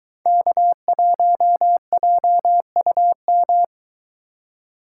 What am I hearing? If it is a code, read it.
Morse code, heard as a keyed tone, about 23 wpm, 700 Hz. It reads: K1JUM